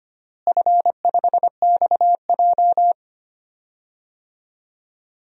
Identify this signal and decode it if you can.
Morse code, heard as a keyed tone, about 25 wpm, 700 Hz. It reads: F5XJ
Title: DXCC-100 Call Signs - 4 Characters - Repeated 2x - 25wpm